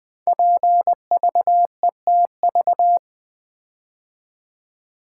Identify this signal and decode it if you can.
Morse code, heard as a keyed tone, about 20 wpm, 700 Hz. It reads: PVETV
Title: Code Groups - 5 Letters - 20wpm